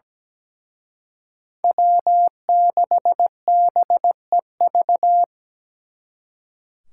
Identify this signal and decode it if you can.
Morse code, heard as a keyed tone, about 20 wpm, 700 Hz. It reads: W6BEV
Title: Speed Racing - USA 1x3 Call Signs 17wpm